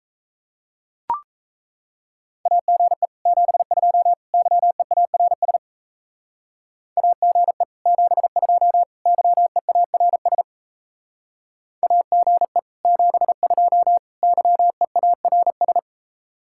Morse code, heard as a keyed tone, about 34 wpm, 700 Hz. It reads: AGE 72 YEARS AGE 72 YEARS AGE 72 YEARS
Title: Speed-Racing Common QSO Phrases - 28wpm